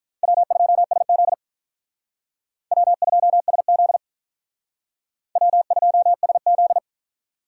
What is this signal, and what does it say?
Morse code, heard as a keyed tone, about 48 wpm, 700 Hz. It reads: W2SZ W2SZ W2SZ